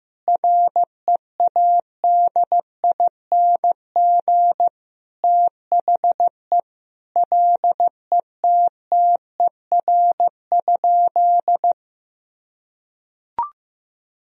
Morse code, heard as a keyed tone, about 15 wpm, 700 Hz. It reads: READING THE LETTER?
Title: Sentences from Top 1000 Words - Review Part 3 - 15wpm